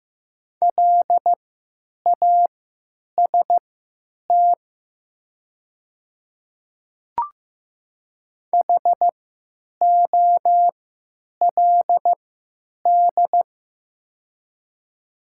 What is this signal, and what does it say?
Morse code, heard as a keyed tone, about 15 wpm, spaced as for 9 wpm, 700 Hz. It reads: LAST HOLD
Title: 3x Character Spacing - Top 500 Words - 15wpm